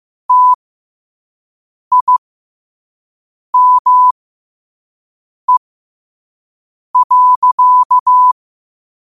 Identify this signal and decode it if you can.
Morse code, heard as a keyed tone, about 15 wpm, spaced as for 5 wpm, 1000 Hz. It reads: TIME.
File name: qotdfiles / 5wpm